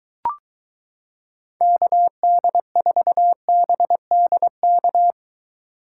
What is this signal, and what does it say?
Morse code, heard as a keyed tone, about 23 wpm, 700 Hz. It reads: KD4BDK